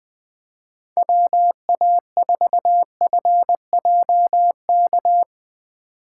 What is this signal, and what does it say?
Morse code, heard as a keyed tone, about 20 wpm, 700 Hz. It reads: WA4FJK
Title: Speed Racing - USA 2x3 Call Signs 17wpm